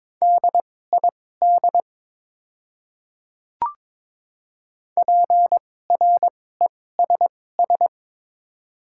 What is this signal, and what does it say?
Morse code, heard as a keyed tone, about 22 wpm, spaced as for 16 wpm, 700 Hz. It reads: DID PRESS
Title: 2x Character Spacing - Top 300 Words - 22wpm